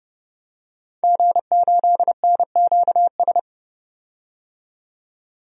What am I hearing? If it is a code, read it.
Morse code, heard as a keyed tone, about 30 wpm, 690 Hz. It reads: G8NQS
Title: DXCC-100 Call Signs - 5 Characters - 30wpm